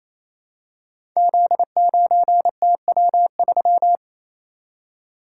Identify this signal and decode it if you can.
Morse code, heard as a keyed tone, about 27 wpm, 700 Hz. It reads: Z9TW3
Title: Code Groups - 5 Letter:Numbers - 28wpm